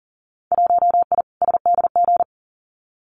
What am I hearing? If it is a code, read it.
Morse code, heard as a keyed tone, about 40 wpm, 700 Hz. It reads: 1I SDG
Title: Winter Field Day Exchanges 40wpm